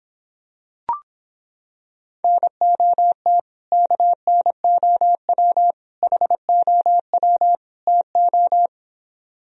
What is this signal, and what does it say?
Morse code, heard as a keyed tone, about 26 wpm, 700 Hz. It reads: NOT KNOW HOW TO